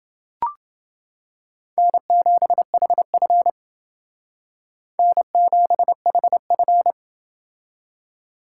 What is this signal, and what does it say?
Morse code, heard as a keyed tone, about 28 wpm, 700 Hz. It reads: N7HF N7HF